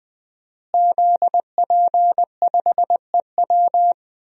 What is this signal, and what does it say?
Morse code, heard as a keyed tone, about 20 wpm, 700 Hz. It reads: ZP5EW
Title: DXCC-100 Call Signs - 5 Characters - Repeated 2x - 20wpm